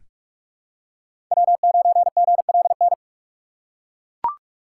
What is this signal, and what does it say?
Morse code, heard as a keyed tone, about 42 wpm, 700 Hz. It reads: W9GLN
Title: USA Call Signs - 5 Characters - 4x Character Spacing - 45wpm